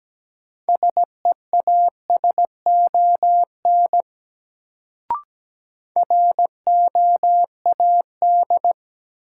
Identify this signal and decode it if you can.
Morse code, heard as a keyed tone, about 17 wpm, 700 Hz. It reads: SEASON ROAD
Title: Sets of 5 Words - New Word Drill - 17wpm